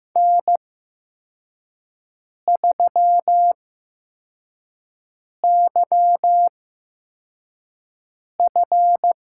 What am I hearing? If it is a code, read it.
Morse code, heard as a keyed tone, about 15 wpm, spaced as for 4 wpm, 700 Hz. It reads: N3YF
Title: USA Call Signs - 5 Characters - 8x Character Spacing - 15wpm